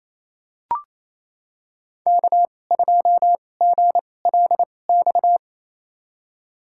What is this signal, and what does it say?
Morse code, heard as a keyed tone, about 28 wpm, spaced as for 20 wpm, 700 Hz. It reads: K2GLX
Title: USA Call Signs - 5 Characters - 2x Character Spacing - 28wpm